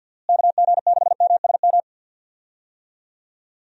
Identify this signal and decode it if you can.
Morse code, heard as a keyed tone, about 51 wpm, 700 Hz. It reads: KG6MSM
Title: USA Call Signs - 6 Characters - Repeated 3x - 50wpm